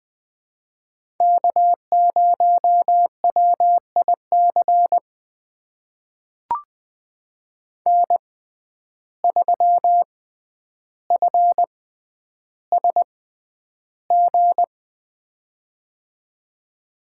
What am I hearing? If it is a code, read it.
Morse code, heard as a keyed tone, about 20 wpm, spaced as for 10 wpm, 700 Hz. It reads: K0WIC N3FSG
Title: USA Call Signs - 5 Characters - 6x Character Spacing - 20wpm